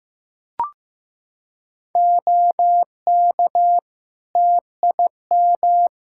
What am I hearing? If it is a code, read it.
Morse code, heard as a keyed tone, about 15 wpm, 700 Hz. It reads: OK TIM
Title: Practice QSO - Exchange 3 - More Info - 15wpm